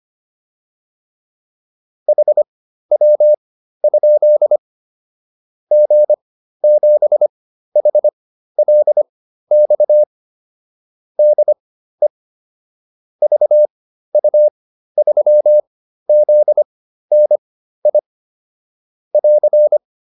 Morse code, heard as a keyed tone, about 25 wpm, spaced as for 13 wpm, 600 Hz. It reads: HW? G7HLX DE VU3ZNI <AR>